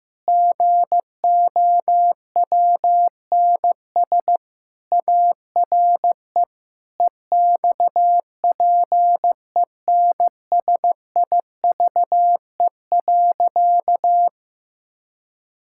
Morse code, heard as a keyed tone, about 16 wpm, 700 Hz. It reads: GOWNS ARE EXPENSIVE.
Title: Pangram Challenge v2 - 15wpm